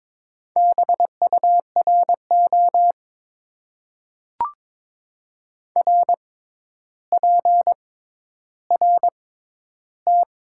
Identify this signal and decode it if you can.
Morse code, heard as a keyed tone, about 21 wpm, spaced as for 11 wpm, 700 Hz. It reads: BURO RPRT
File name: Top 90 QSO Elements - 6x Character Spacing - 22wpm